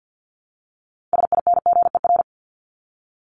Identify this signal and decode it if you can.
Morse code, heard as a keyed tone, about 50 wpm, 700 Hz. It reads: SINGER
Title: Top 1501-1750 Words 50wpm